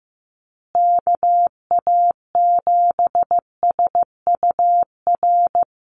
Morse code, heard as a keyed tone, about 15 wpm, 700 Hz. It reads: KA7SUR
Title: USA Call Signs Repeated 2x 15wpm